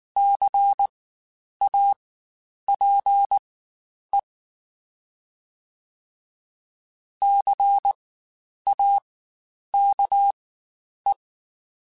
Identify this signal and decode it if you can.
Morse code, heard as a keyed tone, about 19 wpm, spaced as for 9 wpm, 800 Hz. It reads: CAPE CAKE
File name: Spaced Alphas / Spaced Alphas mp3s / Cs sp @18